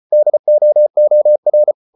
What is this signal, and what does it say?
Morse code, heard as a keyed tone, about 34 wpm, 600 Hz. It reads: DOOR